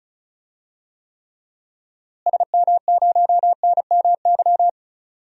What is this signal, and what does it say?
Morse code, heard as a keyed tone, about 36 wpm, 700 Hz. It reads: SM0NMY
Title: DXCC-100 Call Signs - 6 Characters - Repeated 2x - 35wpm